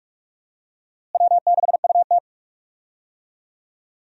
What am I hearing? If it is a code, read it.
Morse code, heard as a keyed tone, about 46 wpm, 700 Hz. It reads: W6UT
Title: USA Call Signs - 4 Characters - Repeated 2x - 45wpm